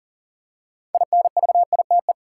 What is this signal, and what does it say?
Morse code, heard as a keyed tone, about 40 wpm, 700 Hz. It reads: INVITE
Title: Six Letter Words - 40wpm